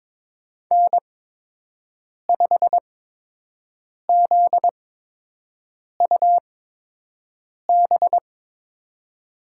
Morse code, heard as a keyed tone, about 22 wpm, spaced as for 6 wpm, 700 Hz. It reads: N5ZUB
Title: USA Call Signs - 5 Characters - 8x Character Spacing - 22wpm